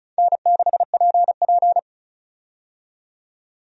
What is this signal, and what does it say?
Morse code, heard as a keyed tone, about 36 wpm, 700 Hz. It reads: N6PP
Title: USA Call Signs - 4 Characters - 35wpm